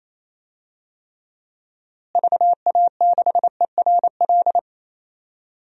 Morse code, heard as a keyed tone, about 28 wpm, 700 Hz. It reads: VA6ERL
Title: DXCC-100 Call Signs - 6 Characters - Repeated 2x - 28wpm